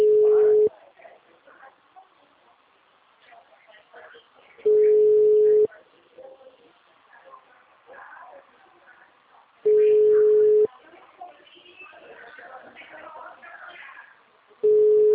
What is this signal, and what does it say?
Ringback tone (Europe, 425 Hz), with speech.